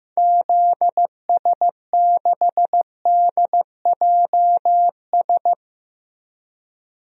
Morse code, heard as a keyed tone, about 16 wpm, 700 Hz. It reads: ZS6DJS
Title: DXCC-100 Call Signs - 6 Characters - Repeated 2x - 15wpm